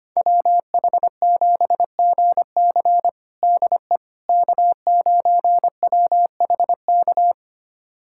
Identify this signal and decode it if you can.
Morse code, heard as a keyed tone, about 25 wpm, 700 Hz. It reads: WH7GC DE K9WHK
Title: USA Callsign DE Callsign - 5 Characters - 25wpm